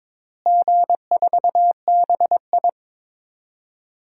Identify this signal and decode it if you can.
Morse code, heard as a keyed tone, about 23 wpm, 700 Hz. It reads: G4BI
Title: DXCC-100 Call Signs - 4 Characters - Repeated 2x - 22wpm